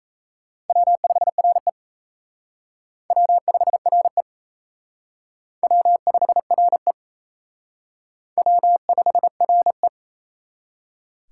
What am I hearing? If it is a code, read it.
Morse code, heard as a keyed tone, about 35 wpm, 700 Hz. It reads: W5RE W5RE W5RE W5RE
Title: Speed Racing - USA 1x2 Call Signs 28wpm